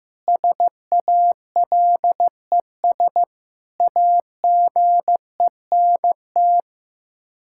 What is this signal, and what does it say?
Morse code, heard as a keyed tone, about 15 wpm, 700 Hz. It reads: SALES AGENT
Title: Practice QSO - Exchange 3 - More Info - 15wpm